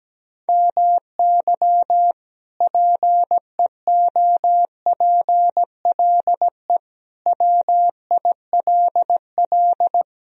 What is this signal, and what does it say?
Morse code, heard as a keyed tone, about 17 wpm, 700 Hz. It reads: MY PEOPLE WILL
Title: Sentences from Top 1000 Words - Review Part 2 - 17wpm